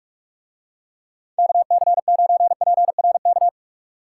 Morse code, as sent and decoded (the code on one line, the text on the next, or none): -.- -.-. ----. .--. .-. -.-
KC9PRK